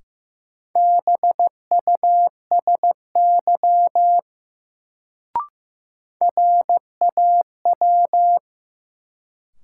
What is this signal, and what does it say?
Morse code, heard as a keyed tone, about 15 wpm, 700 Hz. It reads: BUSY RAW
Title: Sentences from Top 1500 Words - New Word Drill Alt - 15wpm